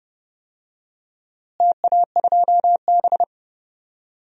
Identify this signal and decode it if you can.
Morse code, heard as a keyed tone, about 30 wpm, 700 Hz. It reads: TA2B
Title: DXCC-100 Call Signs - 4 Characters - Repeated 2x - 30wpm